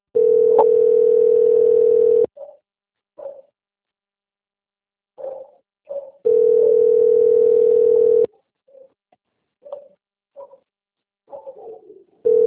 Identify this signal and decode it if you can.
Ringback tone (North America), 440 + 480 Hz.